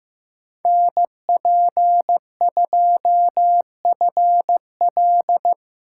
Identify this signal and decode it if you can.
Morse code, heard as a keyed tone, about 18 wpm, 700 Hz. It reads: NP2FL
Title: Speed Racing - USA Call Signs 15wpm